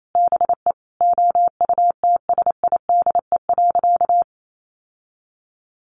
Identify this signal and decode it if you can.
Morse code, heard as a keyed tone, about 28 wpm, 700 Hz. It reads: BE OUTSIDE.
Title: Sentences from Top 1200 Words 28wpm